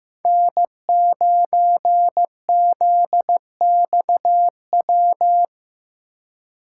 Morse code, heard as a keyed tone, about 15 wpm, 700 Hz. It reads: N9ZXW
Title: USA Call Signs - 5 Characters - 4x Character Spacing - 15wpm